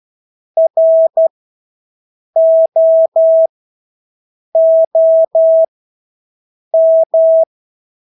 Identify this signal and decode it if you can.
Morse code, heard as a keyed tone, about 12 wpm, spaced as for 6 wpm, 650 Hz. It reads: ROOM
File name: Engelske ord 12-6 wpm